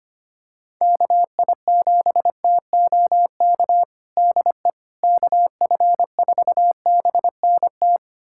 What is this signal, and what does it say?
Morse code, heard as a keyed tone, about 25 wpm, 700 Hz. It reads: KI7TOK DE KF4BNT
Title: USA Callsign DE Callsign - 6 Characters - 25wpm